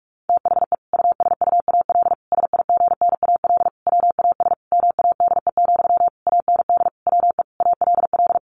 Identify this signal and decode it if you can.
Morse code, heard as a keyed tone, about 44 wpm, 700 Hz. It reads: THE USUAL SIGNAL WAS MADE, AND WE ALL
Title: Sentences from Top 1400 Words 45wpm